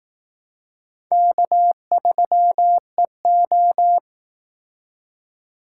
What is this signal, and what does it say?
Morse code, heard as a keyed tone, about 18 wpm, 700 Hz. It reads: K3EO
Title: Speed Racing - USA 1x2 Call Signs 15wpm